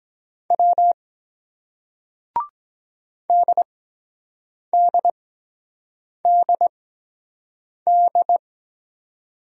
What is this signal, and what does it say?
Morse code, heard as a keyed tone, about 23 wpm, spaced as for 7 wpm, 700 Hz. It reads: W DDDD